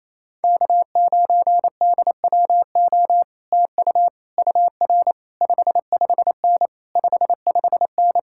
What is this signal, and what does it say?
Morse code, heard as a keyed tone, about 28 wpm, 700 Hz. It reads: K9DWO TU UR 55N 55N
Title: Speed-Racing Common QSO Phrases - 28wpm